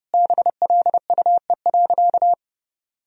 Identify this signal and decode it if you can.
Morse code, heard as a keyed tone, about 32 wpm, 700 Hz. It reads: BLUE.
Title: Sentences from Top 1000 Words - Review Part 3 - 30wpm